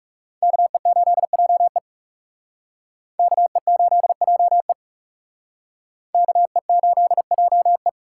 Morse code, heard as a keyed tone, about 38 wpm, 700 Hz. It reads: KE8JE KE8JE KE8JE